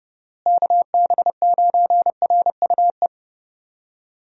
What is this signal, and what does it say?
Morse code, heard as a keyed tone, about 30 wpm, 700 Hz. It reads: KB9RUE
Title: USA Call Signs - 6 Characters - Repeated 3x - 30wpm